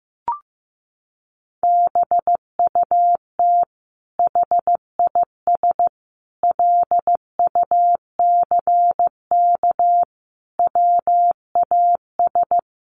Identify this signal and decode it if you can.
Morse code, heard as a keyed tone, about 15 wpm, 700 Hz. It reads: BUT HIS LUCK WAS
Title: Sentences from Top 1500 Words 15wpm